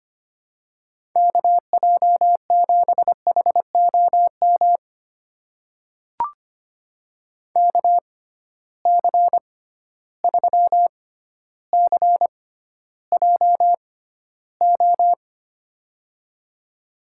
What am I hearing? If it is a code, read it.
Morse code, heard as a keyed tone, about 25 wpm, spaced as for 13 wpm, 700 Hz. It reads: KJ7HOM KC3CJO
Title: USA Call Signs - 6 Characters - 6x Character Spacing - 25wpm